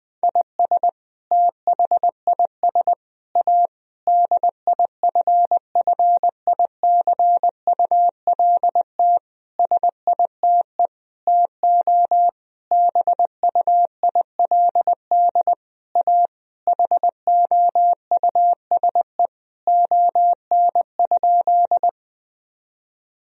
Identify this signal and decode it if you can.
Morse code, heard as a keyed tone, about 21 wpm, 700 Hz. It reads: IS THIS A DIFFICULT SITE TO BUILD A HOUSE ON?
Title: Sentences from Top 1000 Words - Review Part 4 - 20wpm